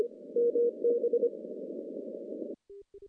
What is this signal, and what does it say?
Morse code, heard as a keyed tone, about 25 wpm, 480 Hz. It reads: MB
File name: CW-SS-k5tr